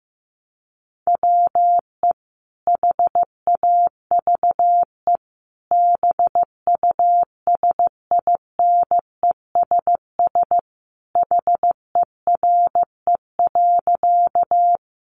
Morse code, heard as a keyed tone, about 15 wpm, 700 Hz. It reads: WE HAVE BUSINESS HERE.
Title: Sentences from Top 600 Words 15wpm